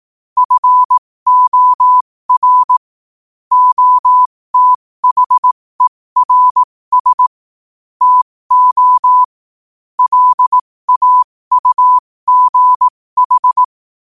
Morse code, heard as a keyed tone, about 18 wpm, spaced as for 15 wpm, 1000 Hz. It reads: FOR OTHERS TO LAUGH
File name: qotdfiles / 15wpm